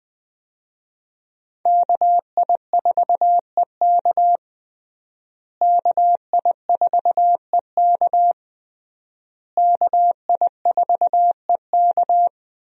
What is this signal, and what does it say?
Morse code, heard as a keyed tone, about 21 wpm, 700 Hz. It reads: KI4EK KI4EK KI4EK
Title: USA Call Signs - 5 Characters - Repeated 3x - 20wpm